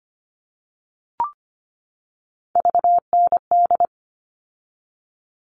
Morse code, heard as a keyed tone, about 25 wpm, 700 Hz. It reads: VND
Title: All Cut Numbers - 3-Digit 25wpm